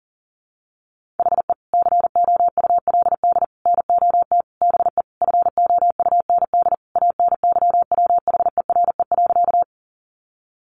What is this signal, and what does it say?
Morse code, heard as a keyed tone, about 40 wpm, 700 Hz. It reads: HE COULD NOT BE FOUND ANYWHERE.
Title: Sentences from Top 1400 Words 40wpm